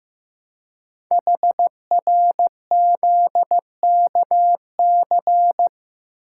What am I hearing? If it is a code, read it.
Morse code, heard as a keyed tone, about 15 wpm, 700 Hz. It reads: HRZKC
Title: Code Groups - 5 Letters - 15wpm